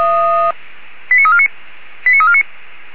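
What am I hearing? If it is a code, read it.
Five-tone selective call (ZVEI), heard as quick runs of five tones, tones of about 70 ms. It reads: 89349 89349
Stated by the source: ZVEI five-tone selective call received by SDR.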